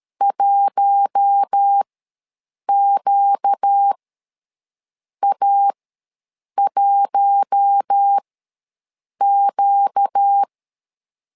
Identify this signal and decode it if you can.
Morse code, heard as a keyed tone, about 13 wpm, spaced as for 6 wpm, 800 Hz. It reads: 1QA1Q
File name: LEZIONI / 4 Q1 ws 12wpm 2129